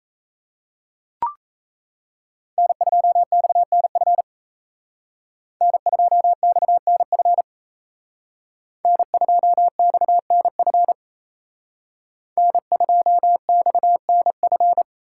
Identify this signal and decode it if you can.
Morse code, heard as a keyed tone, about 33 wpm, 700 Hz. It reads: N2XNF N2XNF N2XNF N2XNF
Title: Speed Racing - USA 1x3 Call Signs 28wpm